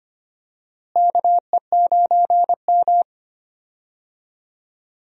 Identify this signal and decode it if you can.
Morse code, heard as a keyed tone, about 25 wpm, 700 Hz. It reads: KE9M
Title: USA Call Signs - 4 Characters - Repeated 3x - 25wpm